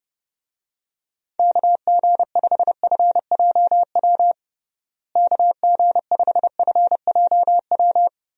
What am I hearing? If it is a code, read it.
Morse code, heard as a keyed tone, about 30 wpm, 700 Hz. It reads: KG5FJW KG5FJW
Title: USA Call Signs - 6 Characters - Repeated 2x - 30wpm